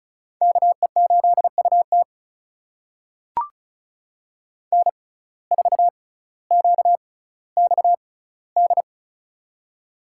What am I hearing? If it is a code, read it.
Morse code, heard as a keyed tone, about 34 wpm, spaced as for 18 wpm, 700 Hz. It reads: KE8UT N4QXD
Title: USA Call Signs - 5 Characters - 6x Character Spacing - 35wpm